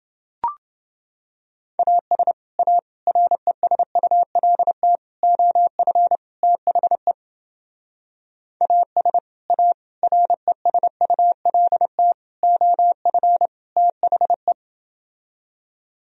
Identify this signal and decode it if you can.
Morse code, heard as a keyed tone, about 29 wpm, 700 Hz. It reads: AS A RESULT OF THE AS A RESULT OF THE